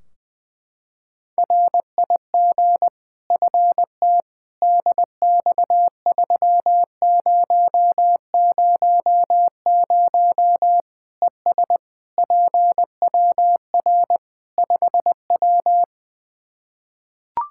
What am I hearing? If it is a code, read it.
Morse code, heard as a keyed tone, about 20 wpm, 700 Hz. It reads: RIG FT DX3000 ES PWR 5W
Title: K1USN SST QSO Practice 20wpm